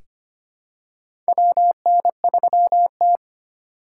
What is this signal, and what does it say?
Morse code, heard as a keyed tone, about 25 wpm, 700 Hz. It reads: WN3T
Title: USA Call Signs - 4 Characters - 3x Character Spacing - 25wpm